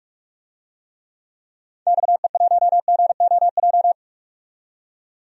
Morse code, heard as a keyed tone, about 46 wpm, 700 Hz. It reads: XE1GOJ
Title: DXCC-100 Call Signs - 6 Characters - 45wpm